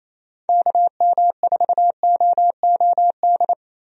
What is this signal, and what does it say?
Morse code, heard as a keyed tone, about 28 wpm, 700 Hz. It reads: KM4OOD
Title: USA Call Signs - 6 Characters - Repeated 3x - 28wpm